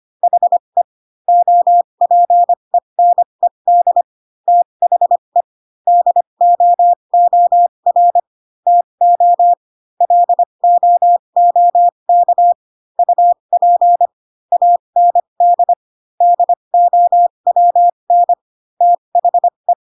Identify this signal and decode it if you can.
Morse code, heard as a keyed tone, about 25 wpm, spaced as for 22 wpm, 700 Hz. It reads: HE OPENED THE DOOR TO LOOK UP AND DOWN THE